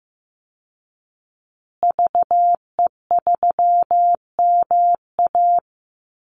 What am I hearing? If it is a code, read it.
Morse code, heard as a keyed tone, about 15 wpm, 700 Hz. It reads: VE3MA
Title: Call Signs Repeated 3x - 15wpm